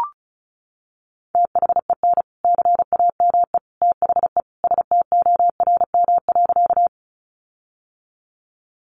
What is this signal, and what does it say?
Morse code, heard as a keyed tone, about 34 wpm, 700 Hz. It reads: THEN CAME THE STORM.
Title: Sentences from Top 1500 Words 35wpm